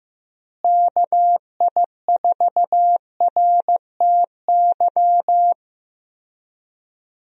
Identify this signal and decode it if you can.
Morse code, heard as a keyed tone, about 15 wpm, 700 Hz. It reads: KI4RTY
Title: USA Call Signs - 6 Characters - Repeated 2x - 15wpm